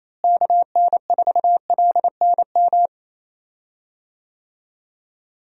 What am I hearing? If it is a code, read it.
Morse code, heard as a keyed tone, about 28 wpm, 700 Hz. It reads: KN4LNM
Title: USA Call Signs - 6 Characters - Repeated 2x - 28wpm